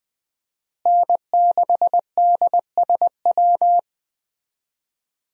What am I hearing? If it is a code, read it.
Morse code, heard as a keyed tone, about 21 wpm, 700 Hz. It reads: N6DSW
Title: USA Call Signs - 5 Characters - Repeated 3x - 20wpm